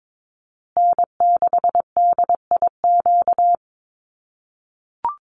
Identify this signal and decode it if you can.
Morse code, heard as a keyed tone, about 22 wpm, 700 Hz. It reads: N6DIQ
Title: USA Call Signs Repeated 3x 22wpm